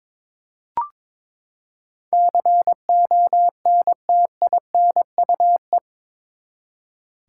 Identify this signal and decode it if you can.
Morse code, heard as a keyed tone, about 22 wpm, 700 Hz. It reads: CONTINUE
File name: practice / 22wpm / Eight Letter Words - 22wpm